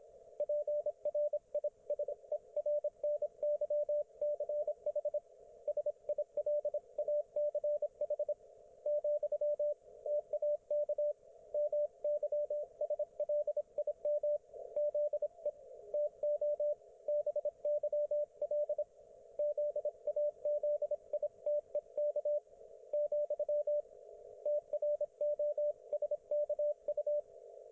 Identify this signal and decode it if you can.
Morse code, heard as a keyed tone, about 25 wpm, 590 Hz, with radio noise. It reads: PRISERNYCH SILACH , TAK MYSLIM ZE TO BYL ZAZITEK , TROSKU
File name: Archiv / CUC194 / ok1fhi cul194